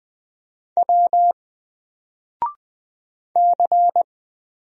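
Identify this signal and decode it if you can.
Morse code, heard as a keyed tone, about 20 wpm, 700 Hz. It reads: W C
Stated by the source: Single letter and number Morse drill at 20 wpm, from N4LCD.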